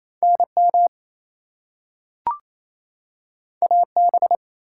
Morse code, heard as a keyed tone, about 28 wpm, 700 Hz. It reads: NM AB
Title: US States and Canadian Provinces 28wpm